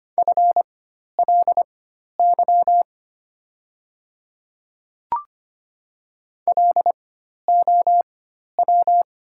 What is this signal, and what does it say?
Morse code, heard as a keyed tone, about 25 wpm, spaced as for 12 wpm, 700 Hz. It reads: FLY LOW
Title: 4x Character Spacing - Top 500 Words - 25wpm